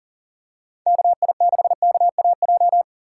Morse code, heard as a keyed tone, about 40 wpm, 690 Hz. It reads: KI6KAJ
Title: USA Call Signs - 3x Character Spacing - 40wpm